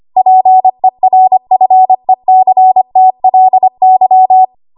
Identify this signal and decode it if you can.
Morse code, heard as a keyed tone, about 25 wpm, 750 Hz. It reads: PERFECTLY
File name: Archive / 25 WPM / 25WPM